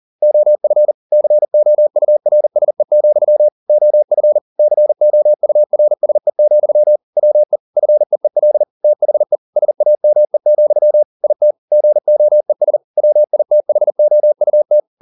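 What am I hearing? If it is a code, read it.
Morse code, heard as a keyed tone, about 40 wpm, 600 Hz. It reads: OF COURSE, OF COURSE, WE FEEL THE SAME, IT GOES WITHOUT
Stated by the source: Morse code at 40 wpm, practice text, CW med Gnister.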